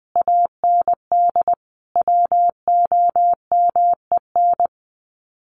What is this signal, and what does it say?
Morse code, heard as a keyed tone, about 20 wpm, 700 Hz. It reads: AND WOMEN
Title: Sets of 3 Words 20wpm